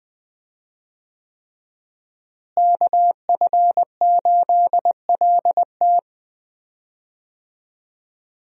Morse code, heard as a keyed tone, about 20 wpm, 700 Hz. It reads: KF8LT
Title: USA Call Signs - 5 Characters - Repeated 3x - 20wpm